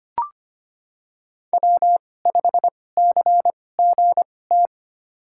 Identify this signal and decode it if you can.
Morse code, heard as a keyed tone, about 25 wpm, spaced as for 18 wpm, 700 Hz. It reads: W5CGT
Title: USA Call Signs - 5 Characters - 2x Character Spacing - 25wpm